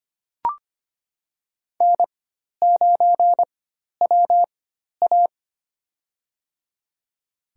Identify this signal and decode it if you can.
Morse code, heard as a keyed tone, about 25 wpm, spaced as for 12 wpm, 700 Hz. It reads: N9WA
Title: USA Call Signs - 4 Characters - 4x Character Spacing - 25wpm